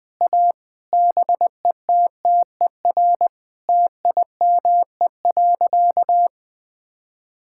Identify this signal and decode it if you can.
Morse code, heard as a keyed tone, about 21 wpm, 700 Hz. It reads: A BETTER TIME.
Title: Sentences with 8-Word Phrases - 20wpm